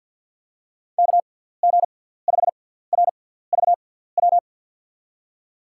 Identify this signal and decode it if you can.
Morse code, heard as a keyed tone, about 53 wpm, spaced as for 17 wpm, 700 Hz. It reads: KG5RVW
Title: USA Call Signs - 6 Characters - 6x Character Spacing - 50wpm